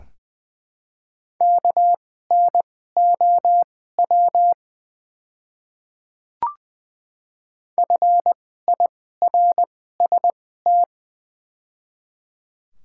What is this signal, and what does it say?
Morse code, heard as a keyed tone, about 20 wpm, spaced as for 14 wpm, 700 Hz. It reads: KNOW FIRST
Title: 2x Character Spacing - Top 100 Words - 20wpm